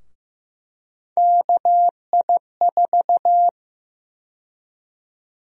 Morse code, heard as a keyed tone, about 15 wpm, 700 Hz. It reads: KI4
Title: USA Call Sign Prefixes - Repeated 2x - 15wpm